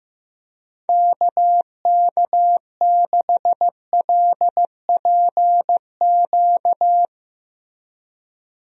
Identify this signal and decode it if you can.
Morse code, heard as a keyed tone, about 15 wpm, 700 Hz. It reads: KK6LPQ
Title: USA Call Signs - 6 Characters - Repeated 2x - 15wpm